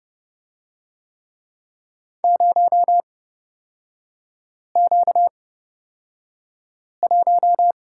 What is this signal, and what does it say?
Morse code, heard as a keyed tone, about 29 wpm, spaced as for 5 wpm, 700 Hz. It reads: 0Q1
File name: Lesson 38e - Letter q - Three Characters 30wpm